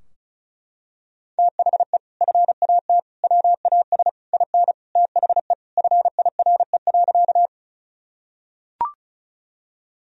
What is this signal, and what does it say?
Morse code, heard as a keyed tone, about 36 wpm, 700 Hz. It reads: THE FAT WAS IN THE FIRE.
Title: Sentences from Top 1000 Words - Review Part 4 - 35wpm